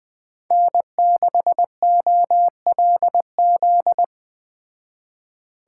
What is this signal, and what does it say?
Morse code, heard as a keyed tone, about 21 wpm, 700 Hz. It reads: N6OLZ
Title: Speed Racing - USA 1x3 Call Signs 17wpm